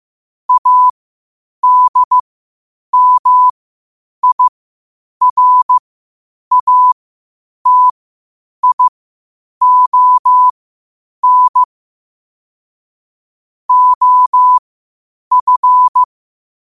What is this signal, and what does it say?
Morse code, heard as a keyed tone, about 15 wpm, spaced as for 8 wpm, 1000 Hz. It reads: ADMIRATION OF
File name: qotdfiles / 8wpm